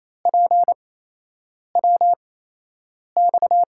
Morse code, heard as a keyed tone, about 28 wpm, spaced as for 8 wpm, 700 Hz. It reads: PWX
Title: USA Call Signs - 8x Character Spacing - 28wpm